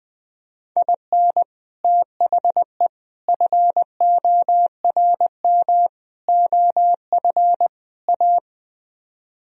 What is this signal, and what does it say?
Morse code, heard as a keyed tone, about 21 wpm, 700 Hz. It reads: IN THE FORM OF A
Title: Speed-Racing - Sets of 5 Words 20wpm